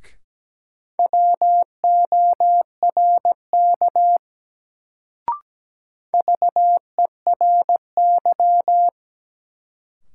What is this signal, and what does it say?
Morse code, heard as a keyed tone, about 17 wpm, 700 Hz. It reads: WORK VERY
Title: Sentences from Top 200 Words - New Word Drill - 17wpm